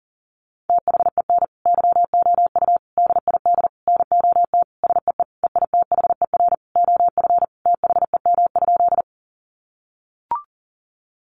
Morse code, heard as a keyed tone, about 40 wpm, 700 Hz. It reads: THEN YOU DID NOT SEE EITHER OF THEM?
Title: Sentences from Top 600 Words 40wpm